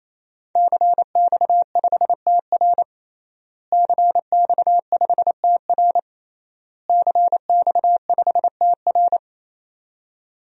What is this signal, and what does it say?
Morse code, heard as a keyed tone, about 28 wpm, 700 Hz. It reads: CX5TR CX5TR CX5TR
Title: DXCC-100 Call Signs - 5 Characters - Repeated 3x - 28wpm